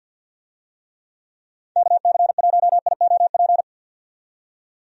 Morse code, heard as a keyed tone, about 45 wpm, 700 Hz. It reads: KC1IOP
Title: USA Call Signs - 6 Characters - 50wpm